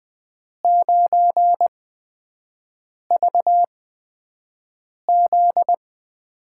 Morse code, heard as a keyed tone, about 20 wpm, spaced as for 5 wpm, 700 Hz. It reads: 9VZ
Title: USA Call Signs - 8x Character Spacing - 20wpm